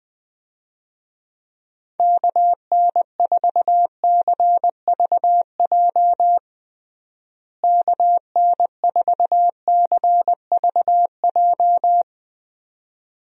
Morse code, heard as a keyed tone, about 21 wpm, 700 Hz. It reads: KN4CVJ KN4CVJ
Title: USA Call Signs - 6 Characters - Repeated 2x - 20wpm